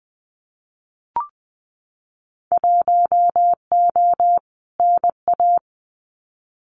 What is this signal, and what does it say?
Morse code, heard as a keyed tone, about 19 wpm, 700 Hz. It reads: 1O NA
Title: Winter Field Day Exchanges 20wpm